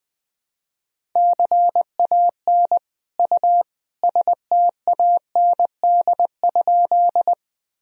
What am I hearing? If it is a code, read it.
Morse code, heard as a keyed tone, about 20 wpm, 700 Hz. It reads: CAN U STAND?
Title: Sentences from Top 300 Words and Abbreviations 20wpm